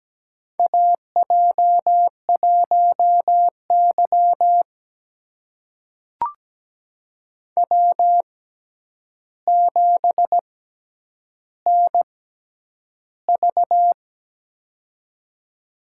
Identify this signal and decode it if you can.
Morse code, heard as a keyed tone, about 17 wpm, spaced as for 9 wpm, 700 Hz. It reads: AJ1Y W7NV
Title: USA Call Signs - 4 Characters - 6x Character Spacing - 17wpm